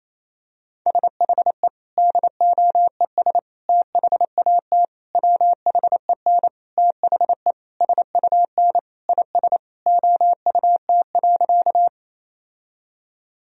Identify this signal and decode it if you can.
Morse code, heard as a keyed tone, about 28 wpm, 700 Hz. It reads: SHE DOES THAT WHEN THE SUN IS OUT.
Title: Sentences from Top 1000 Words - Review Part 1 - 28wpm